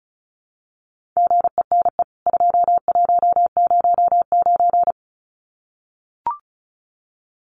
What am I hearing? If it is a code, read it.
Morse code, heard as a keyed tone, about 34 wpm, 700 Hz. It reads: GENE 2109
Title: CWT Exchange 35wpm